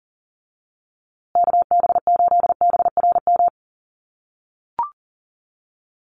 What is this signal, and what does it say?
Morse code, heard as a keyed tone, about 40 wpm, 700 Hz. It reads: KB8BRM
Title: USA Call Signs Repeated 2x 40wpm